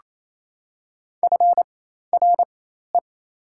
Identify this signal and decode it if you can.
Morse code, heard as a keyed tone, about 28 wpm, spaced as for 13 wpm, 700 Hz. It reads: FRE
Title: 4x Character Spacing - Top 500 Words - 28wpm